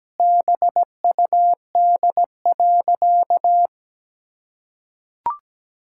Morse code, heard as a keyed tone, about 17 wpm, 700 Hz. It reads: BUD.
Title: Pangram Challenge v2 - 17wpm